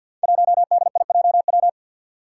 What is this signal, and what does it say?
Morse code, heard as a keyed tone, about 49 wpm, 700 Hz. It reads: 1DIJW
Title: Five Challenging Letter-Number Combinations 50wpm